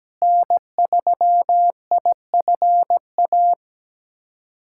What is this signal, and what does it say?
Morse code, heard as a keyed tone, about 17 wpm, 700 Hz. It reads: N3IFA
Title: USA Callsign DE Callsign - 5 Characters - 17wpm